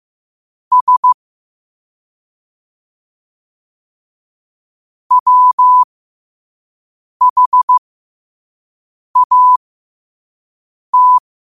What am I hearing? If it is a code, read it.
Morse code, heard as a keyed tone, about 15 wpm, spaced as for 5 wpm, 1000 Hz. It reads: S WHAT
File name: qotdfiles / 5wpm